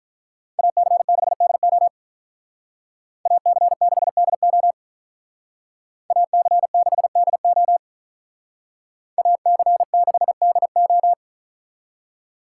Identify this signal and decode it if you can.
Morse code, heard as a keyed tone, about 45 wpm, 700 Hz. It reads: AC6DO AC6DO AC6DO AC6DO